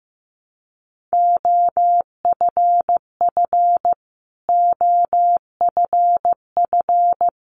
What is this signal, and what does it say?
Morse code, heard as a keyed tone, about 15 wpm, 700 Hz. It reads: OFF OFF
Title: Top 500 Words Repeated 15wpm